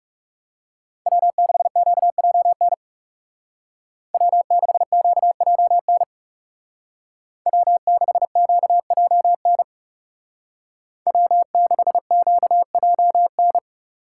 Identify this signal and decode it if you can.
Morse code, heard as a keyed tone, about 35 wpm, 700 Hz. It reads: W6QJN W6QJN W6QJN W6QJN
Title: Speed Racing - USA 1x3 Call Signs 30wpm